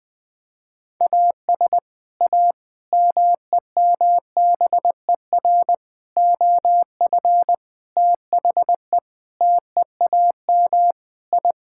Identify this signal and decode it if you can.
Morse code, heard as a keyed tone, about 20 wpm, 700 Hz. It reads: AS A MEMBER OF THE TEAM I
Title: Sentences with 5-Word Phrases - 20wpm